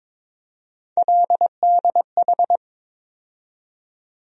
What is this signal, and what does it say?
Morse code, heard as a keyed tone, about 23 wpm, 700 Hz. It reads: LDH
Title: USA Call Sign Postfixes - 3 Letters 22wpm